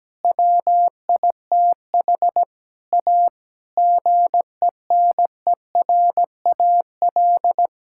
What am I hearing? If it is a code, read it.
Morse code, heard as a keyed tone, about 17 wpm, 700 Hz. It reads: WITH A GENERAL
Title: Sentences from Top 1600 Words 17wpm